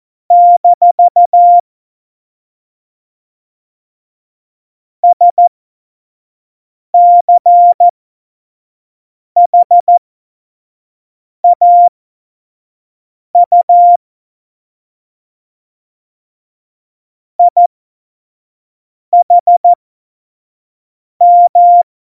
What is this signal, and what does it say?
Morse code, heard as a keyed tone, about 14 wpm, spaced as for 5 wpm, 700 Hz. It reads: - SCHAU IHM
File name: Morsekurs / Witze / Witz 08. Januar 2019, 25 F70